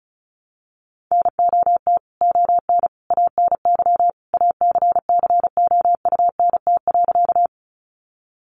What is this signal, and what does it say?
Morse code, heard as a keyed tone, about 34 wpm, 700 Hz. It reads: NOT ON ANY ACCOUNT.
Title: Sentences from Top 900 Words 35wpm